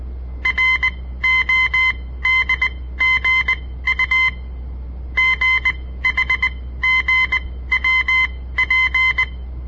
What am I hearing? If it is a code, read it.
Morse code, heard as a keyed tone, about 20 wpm, spaced as for 15 wpm, 2050 Hz. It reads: RODGU GHGWP
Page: On the air Morse code practice from the Southern Montana Amatuer Radio Association; all code, no read back